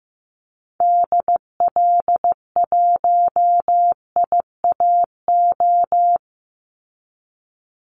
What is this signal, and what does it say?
Morse code, heard as a keyed tone, about 15 wpm, 700 Hz. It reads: DL1IAO
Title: Call Signs Repeated 2x - 15wpm